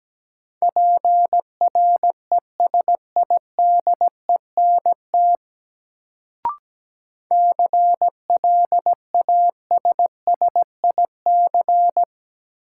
Morse code, heard as a keyed tone, about 17 wpm, 700 Hz. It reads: PRESIDENT CLASSIC
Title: Sentences from Top 1000 Words - New Word Drill - 17wpm